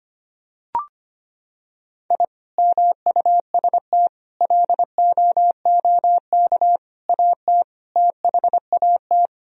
Morse code, heard as a keyed tone, about 25 wpm, 700 Hz. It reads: I MUST LOOK AT THAT
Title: Sentences from Top 1000 Words - Review Part 2 - 25wpm